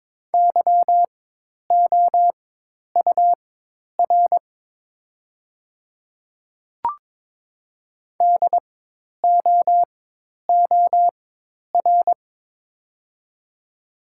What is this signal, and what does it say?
Morse code, heard as a keyed tone, about 22 wpm, spaced as for 10 wpm, 700 Hz. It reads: YOUR DOOR
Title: 4x Character Spacing - Top 300 Words - 22wpm